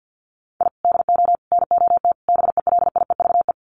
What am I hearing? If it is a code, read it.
Morse code, heard as a keyed tone, about 50 wpm, 700 Hz. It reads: I DO NOT BELIEVE
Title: Sets of 4 Words 50wpm